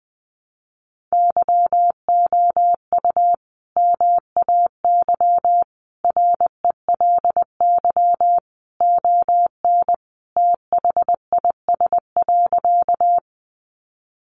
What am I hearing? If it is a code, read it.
Morse code, heard as a keyed tone, about 20 wpm, 700 Hz. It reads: YOU MAY RELY ON THIS.
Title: Sentences from Top 1200 Words 20wpm